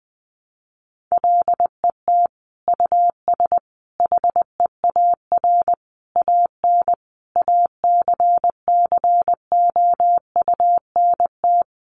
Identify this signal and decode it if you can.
Morse code, heard as a keyed tone, about 20 wpm, 700 Hz. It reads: LET US HEAR AN ACCOUNT
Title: Sentences from Top 900 Words 20wpm